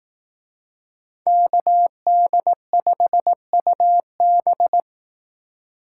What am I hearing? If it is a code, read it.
Morse code, heard as a keyed tone, about 18 wpm, 700 Hz. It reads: KD5UB
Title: Speed Racing - USA 2x2 Call Signs 15wpm